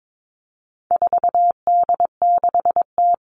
Morse code, heard as a keyed tone, about 22 wpm, 700 Hz. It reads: 4D6T
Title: All Cut Numbers - 4-Digit 22wpm